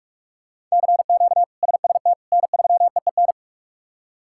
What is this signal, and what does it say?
Morse code, heard as a keyed tone, about 46 wpm, 700 Hz. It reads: CQ SST N3EEN
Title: K1USN SST QSO Practice - 45wpm